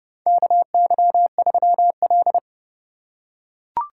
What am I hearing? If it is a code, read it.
Morse code, heard as a keyed tone, about 30 wpm, 700 Hz. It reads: KY3L
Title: USA Call Signs - 4 Characters - 2x Character Spacing - 30wpm